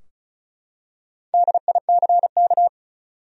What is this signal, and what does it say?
Morse code, heard as a keyed tone, about 34 wpm, 700 Hz. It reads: DICK
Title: USA Common Names - 35wpm